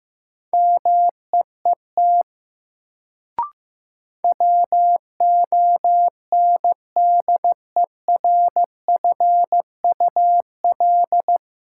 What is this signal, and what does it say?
Morse code, heard as a keyed tone, about 15 wpm, 700 Hz. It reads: MEET WONDERFUL
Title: Sentences from Top 1100 Words - New Word Drill Alt - 15wpm